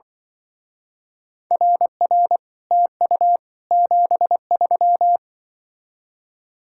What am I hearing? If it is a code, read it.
Morse code, heard as a keyed tone, about 24 wpm, 700 Hz. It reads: RR TU 73
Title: Speed-Racing Common QSO Phrases - 20wpm